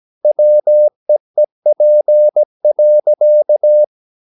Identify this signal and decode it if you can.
Morse code, heard as a keyed tone, about 17 wpm, 600 Hz. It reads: WEEP.
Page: Romanen Never af Ken Follet på engelsk.